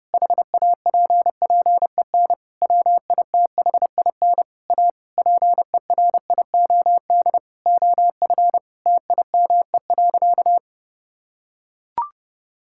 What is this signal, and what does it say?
Morse code, heard as a keyed tone, about 30 wpm, 700 Hz. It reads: HAPPEN WITHIN A PERIOD OF TIME.
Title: Sentences from Top 1000 Words - Review Part 5 - 30wpm